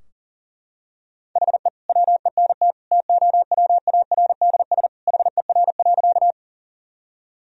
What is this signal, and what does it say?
Morse code, heard as a keyed tone, about 40 wpm, 700 Hz. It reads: HE WENT TOWARDS HER.
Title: Sentences from Top 1000 Words - Review Part 2 - 40wpm